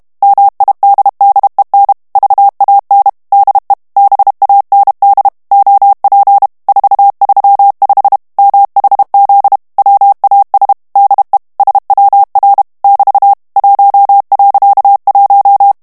Morse code, heard as a keyed tone, about 32 wpm, 800 Hz. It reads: MIDDEN VAN DE BAND OP 435 MHZ WAS DE SWR <BT> 1.1
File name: morse mp3 / HOLLANDAIS texte clair 1 vitesse 30 mots minute